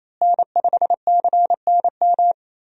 Morse code, heard as a keyed tone, about 27 wpm, 700 Hz. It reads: N5CNM
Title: USA Call Signs - 5 Characters - Repeated 2x - 28wpm